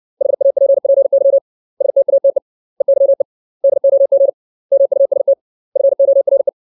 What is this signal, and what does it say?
Morse code, heard as a keyed tone, about 60 wpm, 550 Hz. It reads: HAPPY STATE EYE DOG MUST FORE
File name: 632 words / 632 Words 60WPM